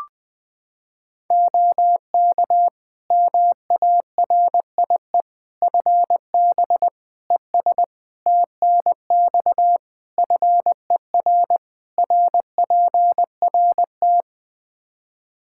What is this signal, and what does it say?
Morse code, heard as a keyed tone, about 19 wpm, 700 Hz. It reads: OK MARIE FB ES TNX FER RPRT
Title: Practice QSO - Exchange 2 - Info - 20wpm